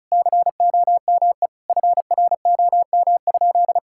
Morse code, heard as a keyed tone, about 36 wpm, 700 Hz. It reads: COME FROM?
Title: Sentences from Top 1000 Words - Review Part 6 - 35wpm